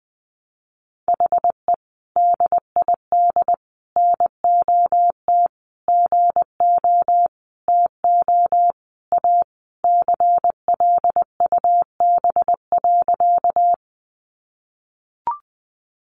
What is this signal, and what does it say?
Morse code, heard as a keyed tone, about 19 wpm, 700 Hz. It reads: HE DID NOT GO TO A CLUB.
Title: Sentences from Top 1200 Words 20wpm